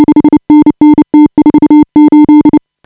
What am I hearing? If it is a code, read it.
Morse code, heard as a keyed tone, about 30 wpm, 340 Hz. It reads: 5NN T48
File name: AUDIO2018 / HA5BA CW